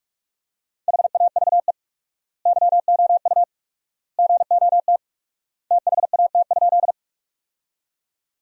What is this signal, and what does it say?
Morse code, heard as a keyed tone, about 42 wpm, 700 Hz. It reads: HAVE YOU GOT THAT?
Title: 4x Word-Spacing - Sentences from Top 300 Words - 45wpm